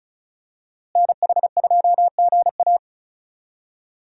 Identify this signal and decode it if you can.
Morse code, heard as a keyed tone, about 36 wpm, 700 Hz. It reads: NH2GA
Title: USA Call Signs - 5 Characters - Repeated 3x - 35wpm